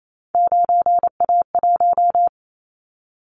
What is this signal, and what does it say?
Morse code, heard as a keyed tone, about 28 wpm, 700 Hz. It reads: 9A1
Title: Callsign Prefixes for Top 50 DXCC - 28wpm